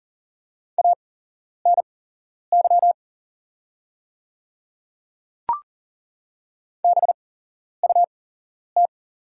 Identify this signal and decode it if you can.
Morse code, heard as a keyed tone, about 39 wpm, spaced as for 11 wpm, 700 Hz. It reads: ANY BUT
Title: 8x Character Spacing - Top 200 Words - 40wpm